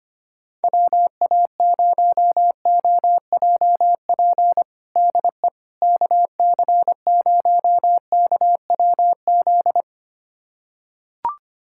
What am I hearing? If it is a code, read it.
Morse code, heard as a keyed tone, about 25 wpm, 700 Hz. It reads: WA0OJP DE KC0KWZ
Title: USA Callsign DE Callsign - 6 Characters - 25wpm